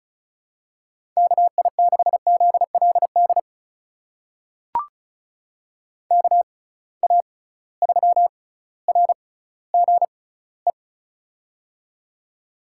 Morse code, heard as a keyed tone, about 36 wpm, spaced as for 18 wpm, 700 Hz. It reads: KI6ZLD KA3RGE
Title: USA Call Signs - 6 Characters - 6x Character Spacing - 35wpm